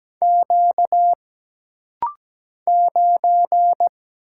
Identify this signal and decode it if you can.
Morse code, heard as a keyed tone, about 17 wpm, 700 Hz. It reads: Q 9
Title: Single Letter-Number - Rapid-Fire 17wpm